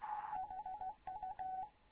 Morse code, heard as a keyed tone, about 16 wpm, 750 Hz. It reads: HU
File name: Morze-HA2MN / hangp / 2 11 GYAKORLAT